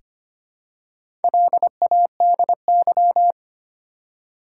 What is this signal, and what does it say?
Morse code, heard as a keyed tone, about 25 wpm, 700 Hz. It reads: LADY
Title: Four Letter Words - 25wpm